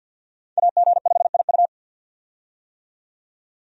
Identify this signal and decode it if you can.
Morse code, heard as a keyed tone, about 50 wpm, 700 Hz. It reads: AG5IU
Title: USA Call Signs - 5 Characters - 50wpm